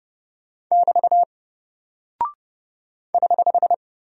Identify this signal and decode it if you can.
Morse code, heard as a keyed tone, about 30 wpm, 700 Hz. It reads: <BT> <HH>
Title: Single Character (Letter:Number:Punctuation:Prosign) 30wpm